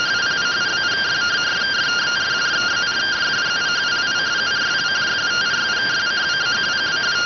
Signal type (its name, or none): two-tone teleprinter signal